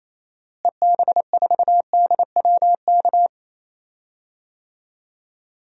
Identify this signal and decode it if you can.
Morse code, heard as a keyed tone, about 27 wpm, 700 Hz. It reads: EB4DWK
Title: DXCC-100 Call Signs - 6 Characters - Repeated 3x - 28wpm